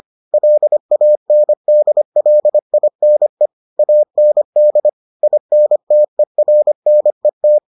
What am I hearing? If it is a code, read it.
Morse code, heard as a keyed tone, about 25 wpm, 600 Hz. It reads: LANDLINE AND INTERNET